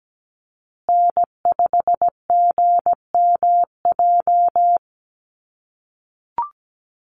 Morse code, heard as a keyed tone, about 17 wpm, 700 Hz. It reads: N5GMJ
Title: USA Call Signs Repeated 2x 17wpm